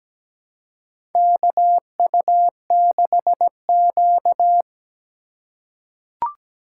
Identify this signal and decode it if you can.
Morse code, heard as a keyed tone, about 17 wpm, 700 Hz. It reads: KU6Q
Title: USA Call Signs - 4 Characters - 2x Character Spacing - 17wpm